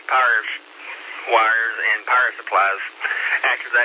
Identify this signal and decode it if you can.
Emergency Alert System (SAME) tones, heard as harsh buzzing bursts, with speech. It is heard from 0.2 to 3.2 s.